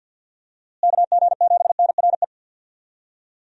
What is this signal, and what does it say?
Morse code, heard as a keyed tone, about 50 wpm, 700 Hz. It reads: KG7NRE
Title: USA Call Signs - 6 Characters - Repeated 3x - 50wpm